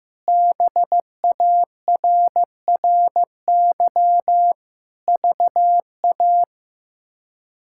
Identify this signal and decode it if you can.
Morse code, heard as a keyed tone, about 16 wpm, 700 Hz. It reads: BARRY VA
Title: K1USN SST QSO Practice - 15wpm